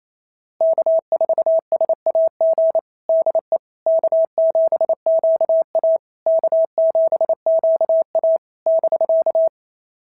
Morse code, heard as a keyed tone, about 27 wpm, 690 Hz. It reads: K4SAG DE K7QA K7QA <BK>
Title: SOTA QSO Practice 28wpm